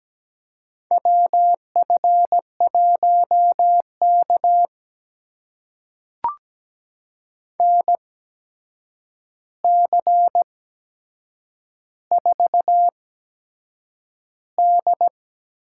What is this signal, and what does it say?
Morse code, heard as a keyed tone, about 17 wpm, spaced as for 7 wpm, 700 Hz. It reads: WF1K NC4D
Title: USA Call Signs - 4 Characters - 8x Character Spacing - 17wpm